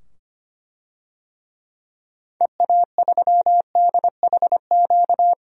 Morse code, heard as a keyed tone, about 25 wpm, 700 Hz. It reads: EA3DHQ
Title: DXCC-100 Call Signs - 6 Characters - 25wpm